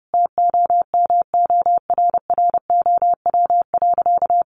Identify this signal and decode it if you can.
Morse code, heard as a keyed tone, about 30 wpm, 700 Hz. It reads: TOMORROW.
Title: Sentences from Top 1100 Words 30wpm